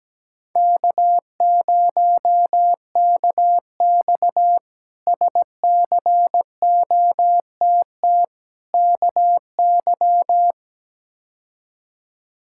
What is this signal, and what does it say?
Morse code, heard as a keyed tone, about 17 wpm, 700 Hz. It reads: K0KX SCOTT KY
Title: K1USN SST QSO Practice - 17wpm